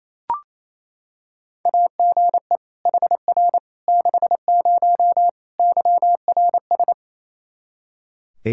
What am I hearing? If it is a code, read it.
Morse code, heard as a keyed tone, about 28 wpm, 700 Hz. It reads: AGE HR 60 YRS
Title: Practice QSO - Exchange 3 - More Info - 28wpm